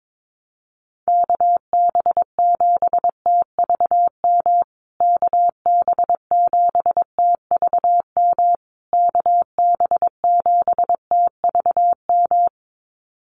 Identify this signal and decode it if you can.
Morse code, heard as a keyed tone, about 22 wpm, 700 Hz. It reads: KB7TVM KB7TVM KB7TVM
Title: USA Call Signs Repeated 3x 22wpm